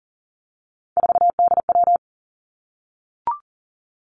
Morse code, heard as a keyed tone, about 40 wpm, 700 Hz. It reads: 4DW
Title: Three Challenging Letter-Number Combinations 40wpm